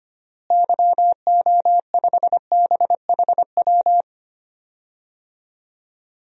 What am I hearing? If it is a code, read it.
Morse code, heard as a keyed tone, about 25 wpm, 700 Hz. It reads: YO5BHW
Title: DXCC-100 Call Signs - 6 Characters - Repeated 2x - 25wpm